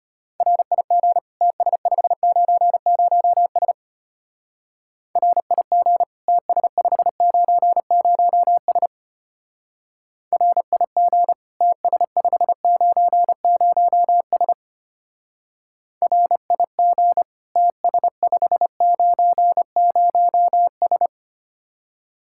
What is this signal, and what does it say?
Morse code, heard as a keyed tone, about 31 wpm, 700 Hz. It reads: RIG TS590S RIG TS590S RIG TS590S RIG TS590S
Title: Speed-Racing Common QSO Phrases - 25wpm